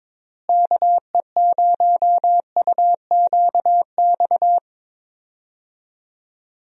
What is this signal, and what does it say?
Morse code, heard as a keyed tone, about 22 wpm, 700 Hz. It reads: KE0UQX
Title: USA Call Signs - 6 Characters - Repeated 3x - 22wpm